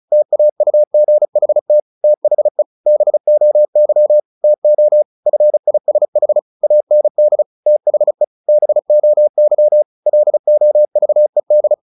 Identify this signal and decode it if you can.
Morse code, heard as a keyed tone, about 34 wpm, 600 Hz. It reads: TAUGHT THE BOY TO FISH AND THE BOY LOVED
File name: the old man and the sea / the old man and the sea 1 35wpm